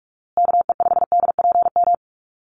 Morse code, heard as a keyed tone, about 44 wpm, 700 Hz. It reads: KE5DPM
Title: USA Call Signs Repeated 2x 45wpm